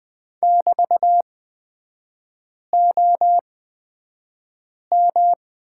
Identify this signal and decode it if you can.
Morse code, heard as a keyed tone, about 20 wpm, spaced as for 5 wpm, 700 Hz. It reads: <BT>OM
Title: 224 Lesson 36e - BT - Three Characters 20wpm